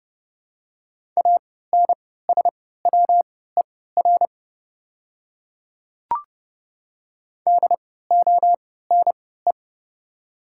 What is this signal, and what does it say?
Morse code, heard as a keyed tone, about 30 wpm, spaced as for 17 wpm, 700 Hz. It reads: ANSWER DONE
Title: 3x Character Spacing - Top 500 Words - 30wpm